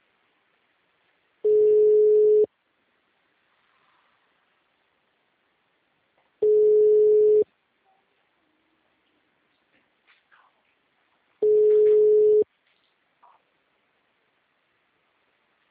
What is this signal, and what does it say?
Ringback tone (Europe, 425 Hz).
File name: out- -2017- - - .0